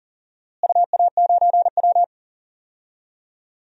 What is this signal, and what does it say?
Morse code, heard as a keyed tone, about 40 wpm, 700 Hz. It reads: UA9W
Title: DXCC-100 Call Signs - 4 Characters - Repeated 2x - 40wpm